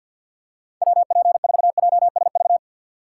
Morse code, heard as a keyed tone, about 52 wpm, 700 Hz. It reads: WP4JSV
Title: USA Call Signs - 6 Characters - 4x Character Spacing - 50wpm